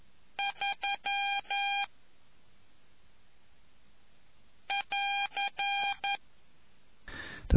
Morse code, heard as a keyed tone, about 11 wpm, 2400 Hz. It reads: 3 <AR>